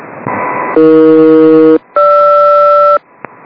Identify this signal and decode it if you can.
Aircraft SELCAL, heard as two pairs of tones. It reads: AE-GR